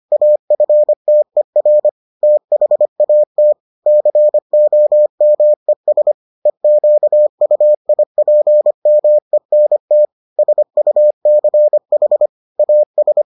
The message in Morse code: AFTER THAT COMES EQUIPMENT SUCH AS

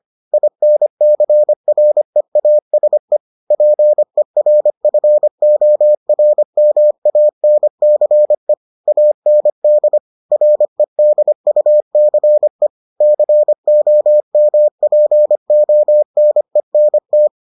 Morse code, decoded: INCREASE PERFORMANCE AND REDUCE COMPONENT

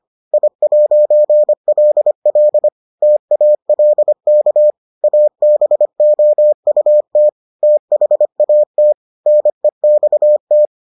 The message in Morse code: I'LL TALK ABOUT THAT NEXT

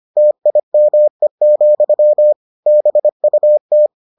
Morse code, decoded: TIME, BUT

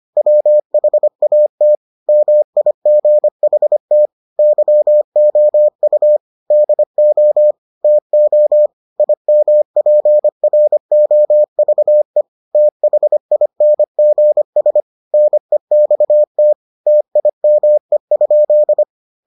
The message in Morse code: WHAT MIGHT YOU DO TO IMPROVE THINGS NEXT TIME?

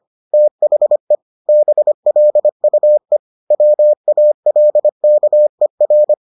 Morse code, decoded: THE BLUE WALKER

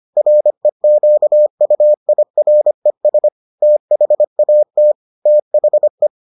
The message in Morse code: REQUIRES THAT THE

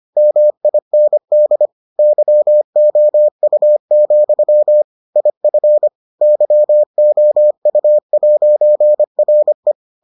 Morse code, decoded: MIND YOU, IF YOU'RE

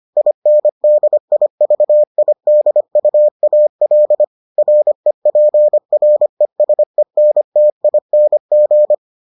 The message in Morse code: INDIVIDUAL REPRESENTING

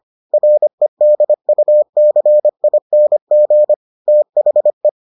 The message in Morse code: REDUCING THE

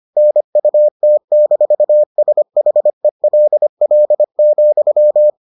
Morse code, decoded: NUT-SHELL,

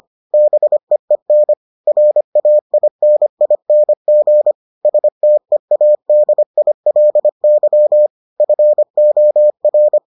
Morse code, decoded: BEEN RAINING STEADILY FOR

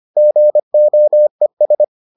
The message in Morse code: GOES

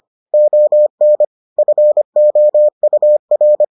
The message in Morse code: ON FOUR

